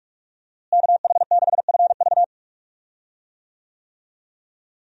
Morse code, heard as a keyed tone, about 46 wpm, 700 Hz. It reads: KH6FV
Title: USA Call Signs - 5 Characters - Repeated 2x - 45wpm